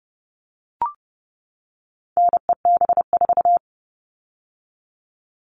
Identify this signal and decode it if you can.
Morse code, heard as a keyed tone, about 30 wpm, 700 Hz. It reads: NEB4
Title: All Cut Numbers - 4-Digit 30wpm